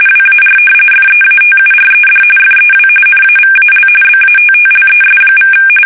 RTTY, about 45.45 baud. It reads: WAR FEATURED FEW NAVAL BATTLES=
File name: fsk850 / usnavy5